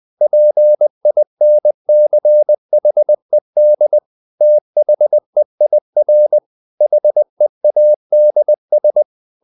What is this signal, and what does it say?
Morse code, heard as a keyed tone, about 20 wpm, 600 Hz. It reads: PINCHED THEIR HEADS
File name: the old man and the sea part 2 20wpm